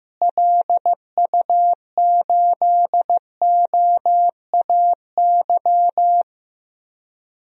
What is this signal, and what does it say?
Morse code, heard as a keyed tone, about 15 wpm, 700 Hz. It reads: LU8OAY
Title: DXCC-100 Call Signs - 6 Characters - Repeated 3x - 15wpm